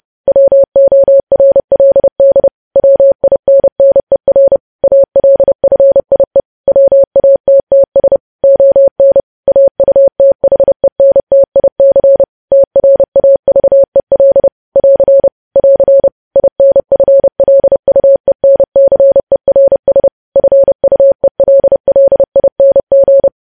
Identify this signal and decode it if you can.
Morse code, heard as a keyed tone, about 30 wpm, 550 Hz. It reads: WORLD WINNER ALFIE WATTS ON AUTHENTIC TRAVEL <AR> <AR> INFLUENCERS FUELLING
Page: news headlines in morse code at 30 wpm